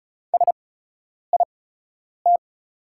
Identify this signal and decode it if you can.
Morse code, heard as a keyed tone, about 36 wpm, spaced as for 10 wpm, 700 Hz. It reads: SIT
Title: 8x Character Spacing - Top 500 Words - 35wpm